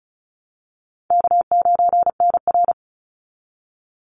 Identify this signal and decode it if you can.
Morse code, heard as a keyed tone, about 34 wpm, 700 Hz. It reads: K9NR
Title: Easy Call Signs Repeated 3x - 35wpm